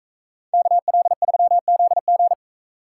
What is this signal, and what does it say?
Morse code, heard as a keyed tone, about 41 wpm, 700 Hz. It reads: KL3ZG